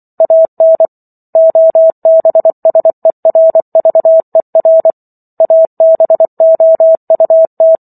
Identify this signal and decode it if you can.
Morse code, heard as a keyed tone, about 24 wpm, 650 Hz. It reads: AN OBSERVER ABOUT